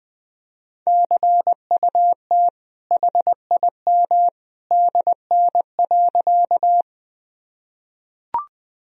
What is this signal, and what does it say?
Morse code, heard as a keyed tone, about 20 wpm, 700 Hz. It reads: CUT HIM DN.
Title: Sentences from Top 300 Words and Abbreviations 20wpm